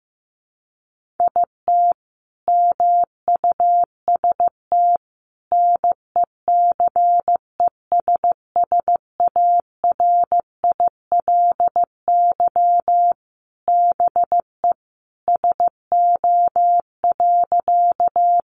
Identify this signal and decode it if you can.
Morse code, heard as a keyed tone, about 15 wpm, 700 Hz. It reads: IT MUST NECESSARILY BE SO.
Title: Sentences from Top 1400 Words 15wpm